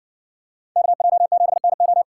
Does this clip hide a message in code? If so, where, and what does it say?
Yes: Morse code from 0.8 to 2.0 s, heard as a keyed tone, about 60 wpm, 700 Hz. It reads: KJ7NO